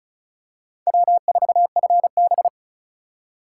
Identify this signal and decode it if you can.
Morse code, heard as a keyed tone, about 36 wpm, 700 Hz. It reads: W4FB
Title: USA Call Signs - 4 Characters - 4x Character Spacing - 35wpm